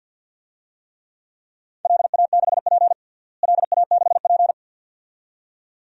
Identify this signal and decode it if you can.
Morse code, heard as a keyed tone, about 53 wpm, 700 Hz. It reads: LA6P LA6P
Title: DXCC-100 Call Signs - 4 Characters - Repeated 2x - 50wpm